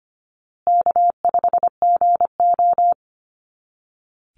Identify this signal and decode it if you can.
Morse code, heard as a keyed tone, about 25 wpm, 700 Hz. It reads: K5GO
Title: Easy Call Signs Repeated 3x - 25wpm